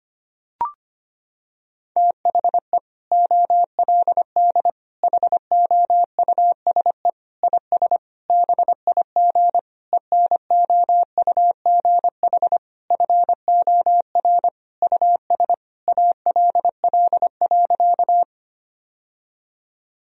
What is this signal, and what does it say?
Morse code, heard as a keyed tone, about 25 wpm, 700 Hz. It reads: THE OLD HOUSE IS BIG ENOUGH FOR US ALL.
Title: Sentences from Top 1000 Words - Review Part 1 - 25wpm